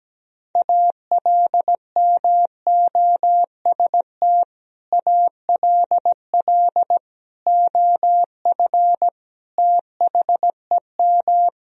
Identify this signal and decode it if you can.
Morse code, heard as a keyed tone, about 17 wpm, 700 Hz. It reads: ALMOST ALL OF THEM
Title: Sentences from Top 1000 Words - Review Part 4 - 17wpm